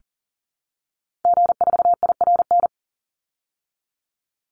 Morse code, heard as a keyed tone, about 40 wpm, 700 Hz. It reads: G4IRN
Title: Call Signs Repeated 3x - 40wpm